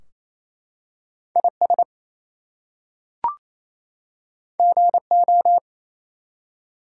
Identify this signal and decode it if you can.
Morse code, heard as a keyed tone, about 27 wpm, 700 Hz. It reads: IS GO
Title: Two Letter Words - 28wpm